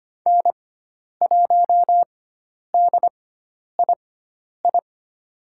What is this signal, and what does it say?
Morse code, heard as a keyed tone, about 25 wpm, spaced as for 10 wpm, 700 Hz. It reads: N1DII
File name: Lesson 16g - Letter c - Call Signs 25wpm